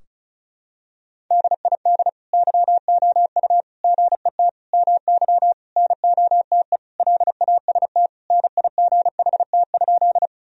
Morse code, heard as a keyed tone, about 34 wpm, 700 Hz. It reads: DID YOU GET MY NOTE LAST NIGHT?
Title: Sentences from Top 1000 Words - Review Part 2 - 35wpm